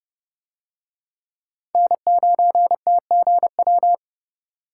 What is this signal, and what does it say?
Morse code, heard as a keyed tone, about 29 wpm, 700 Hz. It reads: N9TGW